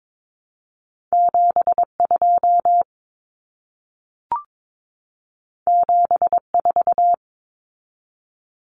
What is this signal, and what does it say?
Morse code, heard as a keyed tone, about 22 wpm, 700 Hz. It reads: 72 74
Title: Common Cut Numbers - 2-Digit 22wpm